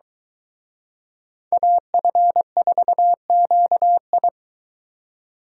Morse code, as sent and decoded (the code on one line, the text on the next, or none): .- ..-. ....- --.- ..
AF4QI